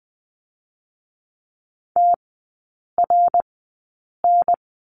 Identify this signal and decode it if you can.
Morse code, heard as a keyed tone, about 20 wpm, spaced as for 8 wpm, 700 Hz. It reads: TRN
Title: 060 Lesson 11e - Letter h - Three Characters 20wpm